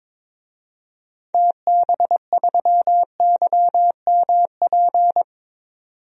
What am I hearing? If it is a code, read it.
Morse code, heard as a keyed tone, about 22 wpm, 700 Hz. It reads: TB3YMP
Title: DXCC-100 Call Signs - 6 Characters - 22wpm